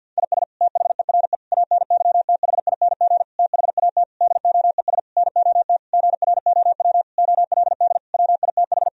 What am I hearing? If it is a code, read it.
Morse code, heard as a keyed tone, about 51 wpm, 700 Hz. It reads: IS THERE ANYTHING THAT DOES NOT GROW OLD WITH
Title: Sentences from Top 1000 Words - Review Part 1 - 50wpm